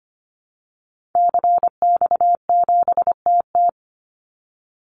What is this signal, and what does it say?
Morse code, heard as a keyed tone, about 25 wpm, 700 Hz. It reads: CX7TT
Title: Call Signs Repeated 3x - 25wpm